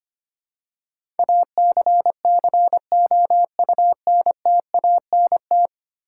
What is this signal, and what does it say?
Morse code, heard as a keyed tone, about 25 wpm, 700 Hz. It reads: ACCOUNTANT
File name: 10 Letter Words - First 3 Letters Given - 25wpm